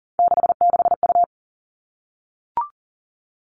Morse code, heard as a keyed tone, about 40 wpm, 700 Hz. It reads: 66U
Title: All Cut Numbers - 3-Digit 40wpm